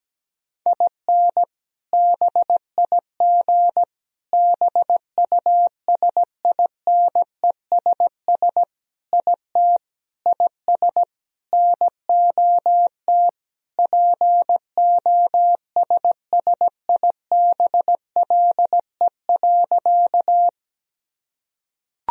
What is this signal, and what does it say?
Morse code, heard as a keyed tone, about 17 wpm, 700 Hz. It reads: IN BIG BUSINESS IT IS NOT POSSIBLE.
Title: Sentences from Top 1000 Words - Review Part 1 - 17wpm